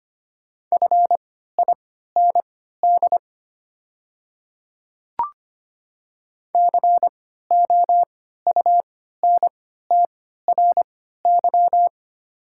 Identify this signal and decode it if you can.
Morse code, heard as a keyed tone, about 25 wpm, spaced as for 14 wpm, 700 Hz. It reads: FIND COUNTRY
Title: 3x Character Spacing - Top 300 Words - 25wpm